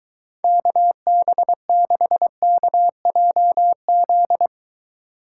Morse code, heard as a keyed tone, about 23 wpm, 700 Hz. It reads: KB6KJZ